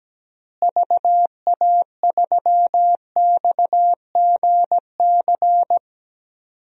Morse code, heard as a keyed tone, about 17 wpm, 700 Hz. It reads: VA3XGC
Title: DXCC-100 Call Signs - 6 Characters - Repeated 3x - 17wpm